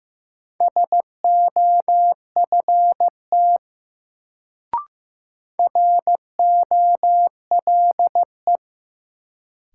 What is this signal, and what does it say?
Morse code, heard as a keyed tone, about 15 wpm, 700 Hz. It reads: SOFT ROLE
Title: Sentences from Top 1500 Words - New Word Drill Alt - 15wpm